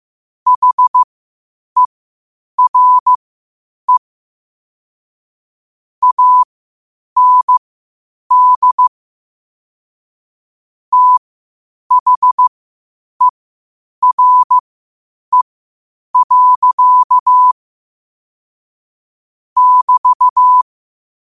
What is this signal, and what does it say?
Morse code, heard as a keyed tone, about 15 wpm, spaced as for 8 wpm, 1000 Hz. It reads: HERE AND THERE. <BT>
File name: qotdfiles / 8wpm